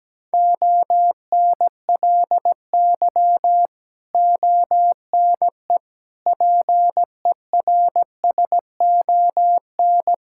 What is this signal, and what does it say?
Morse code, heard as a keyed tone, about 17 wpm, 700 Hz. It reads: ONLY ONE PERSON
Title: Sentences from Top 1000 Words - Review Part 6 - 17wpm